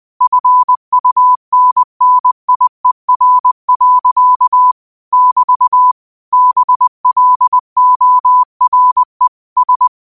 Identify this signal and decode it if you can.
Morse code, heard as a keyed tone, about 20 wpm, 1000 Hz. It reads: FUNNIER. <BT> BLORE S
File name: qotdfiles / 20wpm